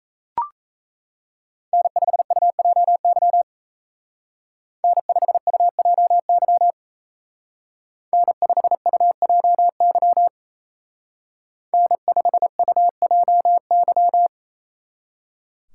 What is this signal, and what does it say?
Morse code, heard as a keyed tone, about 36 wpm, 700 Hz. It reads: N5UJY N5UJY N5UJY N5UJY